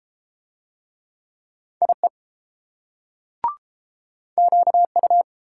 Morse code, heard as a keyed tone, about 30 wpm, 700 Hz. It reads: IE QU